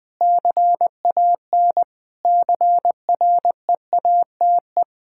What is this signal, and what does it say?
Morse code, heard as a keyed tone, about 20 wpm, 700 Hz. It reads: CAN CREATE
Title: Pangram Challenge v2 - 20wpm